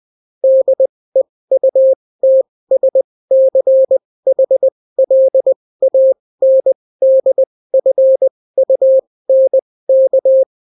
Morse code, heard as a keyed tone, about 20 wpm, spaced as for 16 wpm, 530 Hz. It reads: DEUTSCHLANDFUNK